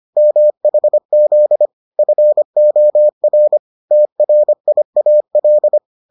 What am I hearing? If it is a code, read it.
Morse code, heard as a keyed tone, about 25 wpm, 600 Hz. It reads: MHZ FOR TRIAL